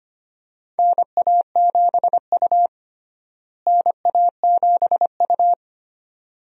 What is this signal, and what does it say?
Morse code, heard as a keyed tone, about 25 wpm, 700 Hz. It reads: NA7U NA7U
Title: USA Call Signs - 4 Characters - Repeated 2x - 25wpm